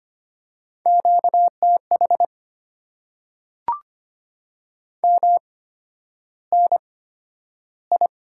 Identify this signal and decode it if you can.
Morse code, heard as a keyed tone, about 25 wpm, spaced as for 11 wpm, 700 Hz. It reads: QTH MNI
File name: Top 90 QSO Elements - 8x Character Spacing - 25wpm